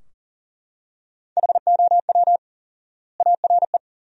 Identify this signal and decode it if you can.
Morse code, heard as a keyed tone, about 37 wpm, 700 Hz. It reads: HOW ARE
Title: 4x Word-Spacing - Sentences from Top 100 Words - 40wpm